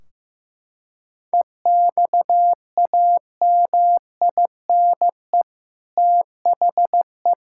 Morse code, heard as a keyed tone, about 16 wpm, 700 Hz. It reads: EXAMINE THE
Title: Sentences from Top 1600 Words 15wpm